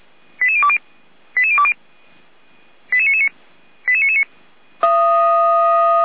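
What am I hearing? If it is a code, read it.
Five-tone selective call (ZVEI), heard as quick runs of five tones, tones of about 70 ms. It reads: 88020 88020 88999 88999
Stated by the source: ZVEI five-tone selective call received by SDR.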